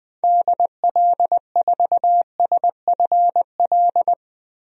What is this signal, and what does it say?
Morse code, heard as a keyed tone, about 21 wpm, 700 Hz. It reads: DL4SFL
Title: DXCC-100 Call Signs - 6 Characters - 20wpm